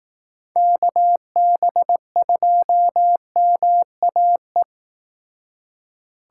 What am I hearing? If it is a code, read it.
Morse code, heard as a keyed tone, about 18 wpm, 700 Hz. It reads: KB2MAE
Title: Speed Racing - USA 2x3 Call Signs 15wpm